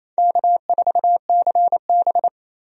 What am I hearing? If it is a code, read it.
Morse code, heard as a keyed tone, about 28 wpm, 700 Hz. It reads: K4CB
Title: USA Call Signs - 4 Characters - 28wpm